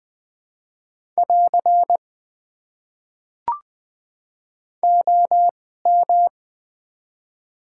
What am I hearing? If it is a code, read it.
Morse code, heard as a keyed tone, about 20 wpm, spaced as for 14 wpm, 700 Hz. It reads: <AR> OM
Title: Top 90 QSO Elements - 2x Character Spacing - 20wpm